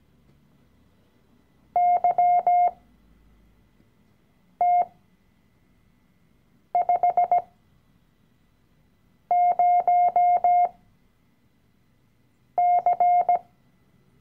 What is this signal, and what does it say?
Morse code, heard as a keyed tone, about 17 wpm, spaced as for 4 wpm, 710 Hz. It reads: YT50C
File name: CW-Online-Kurs / Anfänger / Lektion 012